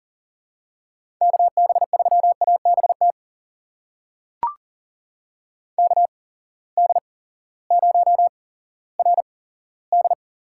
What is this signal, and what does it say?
Morse code, heard as a keyed tone, about 40 wpm, spaced as for 18 wpm, 700 Hz. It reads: KB3ABT KD0RD
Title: USA Call Signs - 6 Characters - 8x Character Spacing - 40wpm